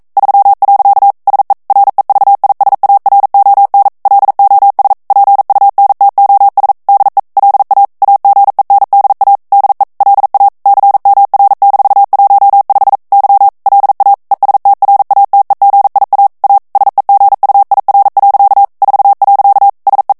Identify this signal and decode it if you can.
Morse code, heard as a keyed tone, about 42 wpm, 800 Hz. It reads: 3. SE REVISARON LOS PUNTOS DE LA AGENDA DE LA CMR-15 Y LA ESTRATEGIA A SEGUIR. 4. SE
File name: morse mp3 / ESPAGNOL TEXTE EN CLAIR 1 VITESSE 40 MOTS MINUTES